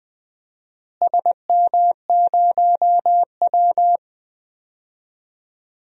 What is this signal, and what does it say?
Morse code, heard as a keyed tone, about 20 wpm, 700 Hz. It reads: SM0W
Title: DXCC-100 Call Signs - 4 Characters - Repeated 2x - 20wpm